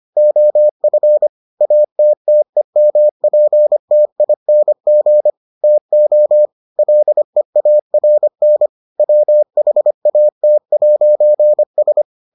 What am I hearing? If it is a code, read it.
Morse code, heard as a keyed tone, about 25 wpm, 600 Hz. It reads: OF ATTEMPTING TO LEARN WHAT'S